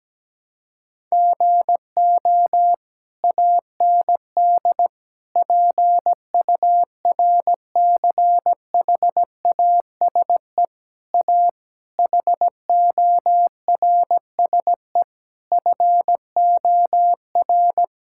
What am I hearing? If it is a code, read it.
Morse code, heard as a keyed tone, about 17 wpm, 700 Hz. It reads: GO AND PURCHASE A HORSE FOR
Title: Sentences from Top 1000 Words - Review Part 5 - 17wpm